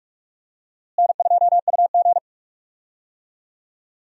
Morse code, heard as a keyed tone, about 41 wpm, 700 Hz. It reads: N2UG
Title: USA Call Signs - 4 Characters - Repeated 3x - 45wpm